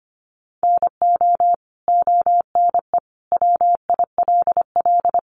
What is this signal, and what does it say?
Morse code, heard as a keyed tone, about 25 wpm, 700 Hz. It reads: NO ONE WILL
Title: Sentences from Top 100 Words 25wpm